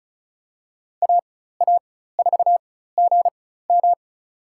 Morse code, heard as a keyed tone, about 34 wpm, spaced as for 16 wpm, 700 Hz. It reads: AA4GM
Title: USA Call Signs - 5 Characters - 4x Character Spacing - 35wpm